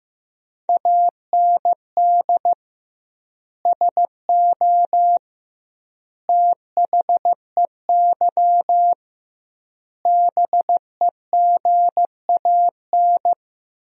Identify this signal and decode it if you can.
Morse code, heard as a keyed tone, about 16 wpm, 700 Hz. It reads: AND SO THEY BEGAN
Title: 2x Word-Spacing - Sentences from Top 400 Words - 15wpm